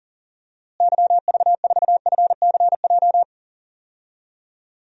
Morse code, heard as a keyed tone, about 40 wpm, 700 Hz. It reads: YV4FCJ
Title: DXCC-100 Call Signs - 6 Characters - Repeated 3x - 40wpm